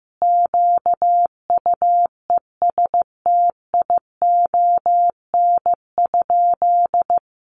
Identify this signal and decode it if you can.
Morse code, heard as a keyed tone, about 15 wpm, 700 Hz. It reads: QUESTION?
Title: Sentences from Top 600 Words 15wpm